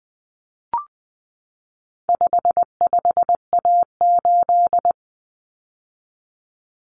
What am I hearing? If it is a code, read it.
Morse code, heard as a keyed tone, about 20 wpm, 700 Hz. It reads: 55A8
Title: Common Cut Numbers - 4-Digit 20wpm